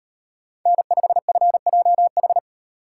Morse code, heard as a keyed tone, about 35 wpm, 700 Hz. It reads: N5FJH